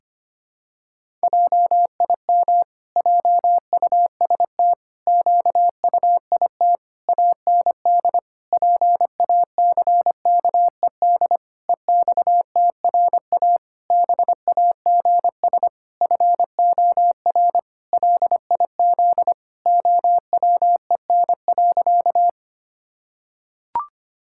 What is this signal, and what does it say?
Morse code, heard as a keyed tone, about 25 wpm, 700 Hz. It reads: JIM JUST QUIT AND PACKED EXTRA BAGS FOR LIZ OWEN.
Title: Pangram Challenge v2 - 25wpm